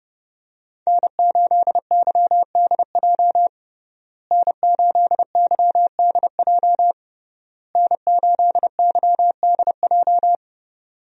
Morse code, heard as a keyed tone, about 30 wpm, 700 Hz. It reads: N8YDJ N8YDJ N8YDJ
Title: USA Call Signs - 5 Characters - Repeated 3x - 30wpm